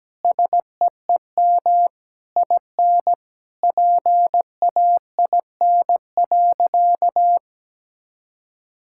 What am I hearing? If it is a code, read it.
Morse code, heard as a keyed tone, about 17 wpm, 700 Hz. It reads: SEEM IN PAIN.
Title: Sentences from Top 1600 Words 17wpm